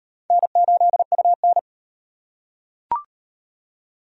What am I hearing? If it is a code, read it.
Morse code, heard as a keyed tone, about 37 wpm, 700 Hz. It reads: N8UN